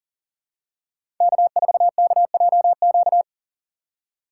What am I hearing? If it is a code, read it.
Morse code, heard as a keyed tone, about 40 wpm, 700 Hz. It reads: K4KJQ
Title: USA Call Signs - 6x Character Spacing - 40wpm